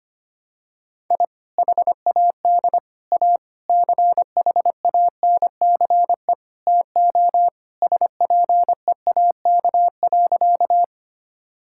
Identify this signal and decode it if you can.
Morse code, heard as a keyed tone, about 25 wpm, 700 Hz. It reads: I HAD A CHANCE TO SPEAK.
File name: practice / 25wpm / Sentences with 4-Word Phrases - 25wpm